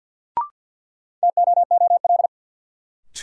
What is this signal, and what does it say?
Morse code, heard as a keyed tone, about 47 wpm, 700 Hz. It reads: TOOL
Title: Sentences from Top 1000 Words - Review Part 6 - New Word Drill - 50wpm